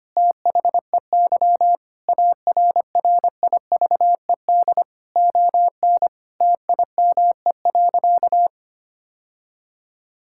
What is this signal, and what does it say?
Morse code, heard as a keyed tone, about 25 wpm, 700 Hz. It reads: THEY ARRIVED ON TIME.
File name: practice / 25wpm / Sentences with 2-Word Phrases - 25wpm